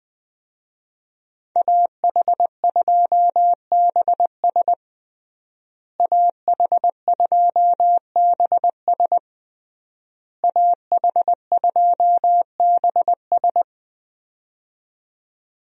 Morse code, heard as a keyed tone, about 20 wpm, 700 Hz. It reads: AH2BS AH2BS AH2BS
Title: USA Call Signs - 5 Characters - Repeated 3x - 20wpm